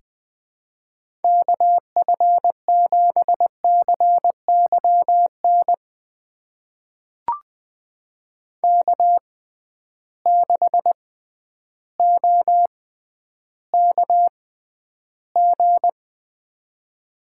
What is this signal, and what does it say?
Morse code, heard as a keyed tone, about 20 wpm, spaced as for 11 wpm, 700 Hz. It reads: KF7CYN K6OKG
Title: USA Call Signs - 6x Character Spacing - 20wpm